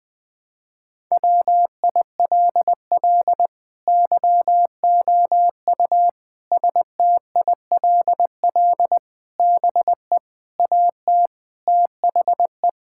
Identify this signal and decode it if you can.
Morse code, heard as a keyed tone, about 20 wpm, 700 Hz. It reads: WILL YOU STILL BE AT THE
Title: Sentences from Top 1000 Words - Review Part 5 - 20wpm